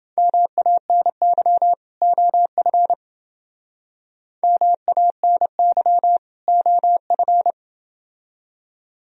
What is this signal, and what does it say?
Morse code, heard as a keyed tone, about 28 wpm, 700 Hz. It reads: MANY OF MANY OF